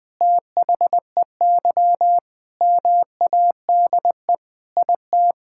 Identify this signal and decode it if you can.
Morse code, heard as a keyed tone, about 20 wpm, 700 Hz. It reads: THEY MADE IT
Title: Sentences with 4-Word Phrases - 20wpm